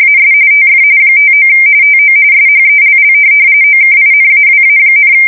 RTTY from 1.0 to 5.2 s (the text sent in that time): 59 TO SHELL OKINAWA,